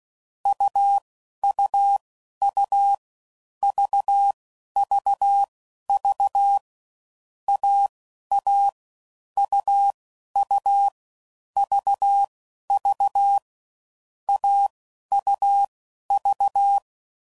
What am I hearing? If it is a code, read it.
Morse code, heard as a keyed tone, about 16 wpm, spaced as for 11 wpm, 800 Hz. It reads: UUU VVV AA UU VV AUV
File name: Combinations / Difficult Combos / AUV Short